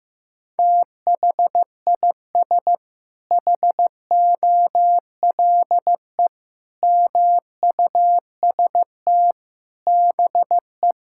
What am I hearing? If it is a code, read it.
Morse code, heard as a keyed tone, about 16 wpm, 700 Hz. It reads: THIS HOLE MUST BE
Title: Sentences from Top 1600 Words 15wpm